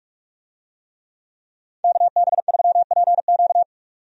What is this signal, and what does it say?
Morse code, heard as a keyed tone, about 43 wpm, 700 Hz. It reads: KB3PQ
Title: USA Call Signs - 5 Characters - 45wpm